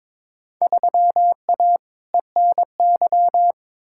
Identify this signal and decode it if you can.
Morse code, heard as a keyed tone, about 22 wpm, 700 Hz. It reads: 3A ENY
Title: Field Day Exchanges - 22wpm